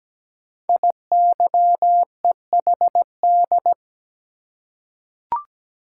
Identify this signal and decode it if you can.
Morse code, heard as a keyed tone, about 17 wpm, 700 Hz. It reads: IYEHD
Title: Code Groups - 5 Letters - 17wpm